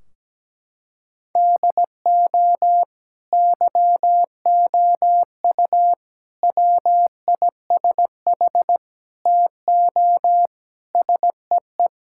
Morse code, heard as a keyed tone, about 17 wpm, 700 Hz. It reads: DO YOU WISH TO SEE
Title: Sentences from Top 1600 Words 17wpm